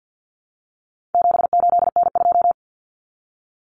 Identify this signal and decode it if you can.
Morse code, heard as a keyed tone, about 50 wpm, 700 Hz. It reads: 78N2
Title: Common Cut Numbers - 4-Digit 50wpm